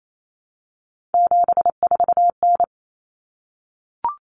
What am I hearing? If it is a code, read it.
Morse code, heard as a keyed tone, about 28 wpm, 700 Hz. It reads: 74N
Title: Common Cut Numbers - 3-Digit 28wpm